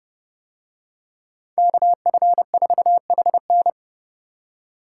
Morse code, heard as a keyed tone, about 29 wpm, 700 Hz. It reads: KF4HN